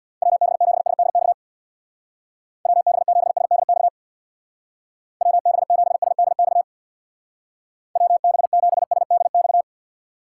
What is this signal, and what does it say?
Morse code, heard as a keyed tone, about 58 wpm, 700 Hz. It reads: WB7SDX WB7SDX WB7SDX WB7SDX
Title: Speed Racing - USA 2x3 Call Signs 50wpm